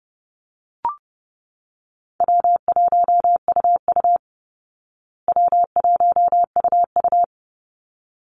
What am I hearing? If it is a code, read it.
Morse code, heard as a keyed tone, about 28 wpm, 700 Hz. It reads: W1UU W1UU
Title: Easy Call Signs Repeated 2x - 30wpm